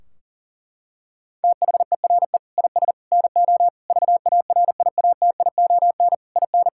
Morse code, heard as a keyed tone, about 40 wpm, 700 Hz. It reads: THERE IS NO VARIATION IN
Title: Sentences from Top 1600 Words 40wpm